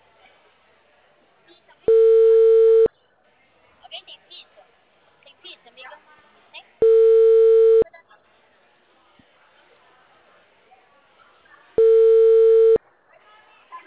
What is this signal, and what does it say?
Ringback tone (China, 450 Hz), with speech.